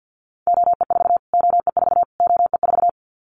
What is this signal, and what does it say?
Morse code, heard as a keyed tone, about 50 wpm, 700 Hz. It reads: OE4 OE4 OE4
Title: Callsign Prefixes for Top 25 DXCC - Repeated 3x - 50wpm